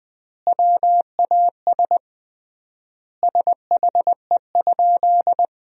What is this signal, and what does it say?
Morse code, heard as a keyed tone, about 20 wpm, 700 Hz. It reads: WAS SHE?
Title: 3x Word-Spacing - Sentences from Top 100 Words - 20wpm